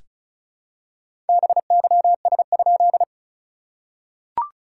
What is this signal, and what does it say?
Morse code, heard as a keyed tone, about 36 wpm, 700 Hz. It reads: BYS?
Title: SOTA QSO Practice 35wpm